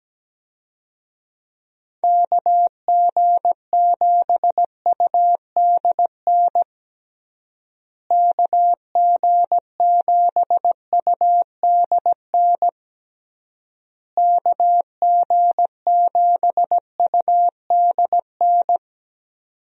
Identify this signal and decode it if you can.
Morse code, heard as a keyed tone, about 17 wpm, 700 Hz. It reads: KG7UDN KG7UDN KG7UDN
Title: USA Call Signs - 6 Characters - Repeated 3x - 17wpm